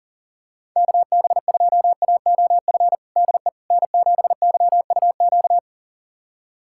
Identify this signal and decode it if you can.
Morse code, heard as a keyed tone, about 40 wpm, 700 Hz. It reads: KB2AOF DE N7YUQ
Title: USA Callsign DE Callsign 40wpm